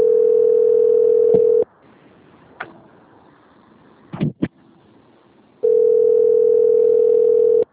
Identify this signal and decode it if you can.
Ringback tone (North America), 440 + 480 Hz.
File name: out- - - - - .22156